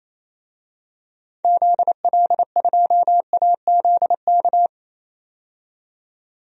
Morse code, heard as a keyed tone, about 28 wpm, 700 Hz. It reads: ZL2AZK
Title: DXCC-100 Call Signs - 6 Characters - 28wpm